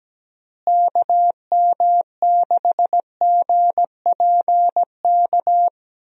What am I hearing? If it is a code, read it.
Morse code, heard as a keyed tone, about 17 wpm, 700 Hz. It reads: KM6GPK
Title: USA Call Signs - 6 Characters - Repeated 2x - 17wpm